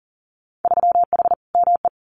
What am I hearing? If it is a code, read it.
Morse code, heard as a keyed tone, about 40 wpm, 700 Hz. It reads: 3H ME
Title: Winter Field Day Exchanges Encore 40wpm